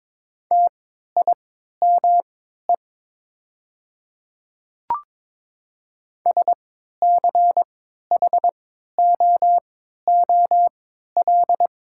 Morse code, heard as a keyed tone, about 23 wpm, spaced as for 13 wpm, 690 Hz. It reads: TIME SCHOOL
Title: 3x Character Spacing - Top 300 Words - 22wpm